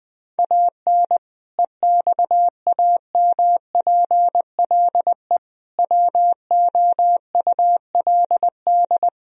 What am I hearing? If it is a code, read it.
Morse code, heard as a keyed tone, about 20 wpm, 700 Hz. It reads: AN EXAMPLE WOULD
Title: Sentences from Top 1000 Words - Review Part 1 - 20wpm